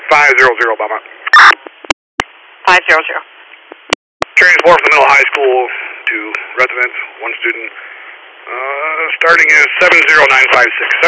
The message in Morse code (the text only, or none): none